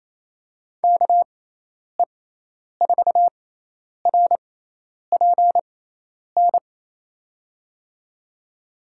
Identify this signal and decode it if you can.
Morse code, heard as a keyed tone, about 28 wpm, spaced as for 10 wpm, 700 Hz. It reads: KE4RPN
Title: USA Call Signs - 6x Character Spacing - 28wpm